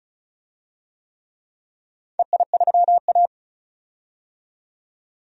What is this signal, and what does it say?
Morse code, heard as a keyed tone, about 36 wpm, 700 Hz. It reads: EI3A
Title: DXCC-100 Call Signs - 4 Characters - 35wpm